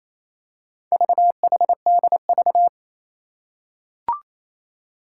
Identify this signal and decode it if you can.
Morse code, heard as a keyed tone, about 29 wpm, 700 Hz. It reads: VHDV
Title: Four Challenging Letter-Number Combinations 28wpm